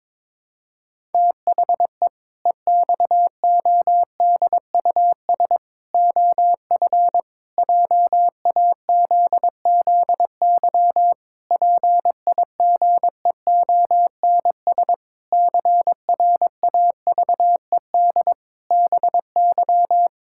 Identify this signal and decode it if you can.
Morse code, heard as a keyed tone, about 22 wpm, 700 Hz. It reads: THE EXODUS OF JAZZY PIGEONS CRAVED BY
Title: Pangram Challenge v2 - 22wpm